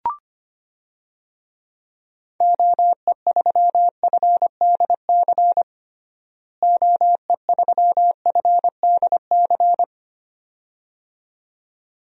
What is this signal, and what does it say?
Morse code, heard as a keyed tone, about 25 wpm, 700 Hz. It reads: OE3FDC OE3FDC
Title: DXCC-100 Call Signs - 6 Characters - Repeated 2x - 25wpm